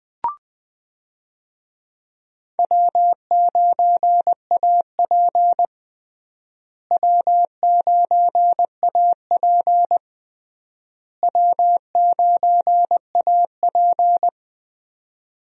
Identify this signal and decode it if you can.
Morse code, heard as a keyed tone, about 20 wpm, 700 Hz. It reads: W9AP W9AP W9AP
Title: USA Call Signs - 4 Characters - Repeated 3x - 20wpm